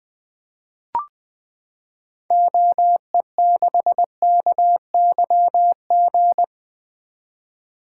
Morse code, heard as a keyed tone, about 20 wpm, 700 Hz. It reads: OE6KYG
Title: 2023 SOTA Activator Call Signs - 20wpm